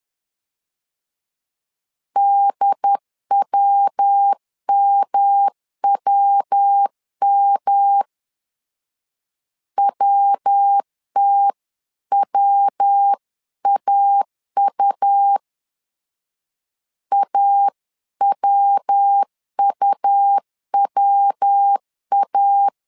Morse code, heard as a keyed tone, about 10 wpm, 800 Hz. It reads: DWMWM WTWAU AWUWA
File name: LEZIONI / 3 MWRO / 3 MW 10wpm 1959